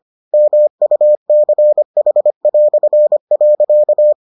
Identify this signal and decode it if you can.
Morse code, heard as a keyed tone, about 25 wpm, 600 Hz. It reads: MUCH".